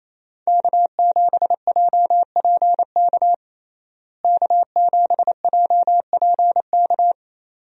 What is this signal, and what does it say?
Morse code, heard as a keyed tone, about 28 wpm, 700 Hz. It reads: K7JPK K7JPK
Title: USA Call Signs - 5 Characters - Repeated 2x - 28wpm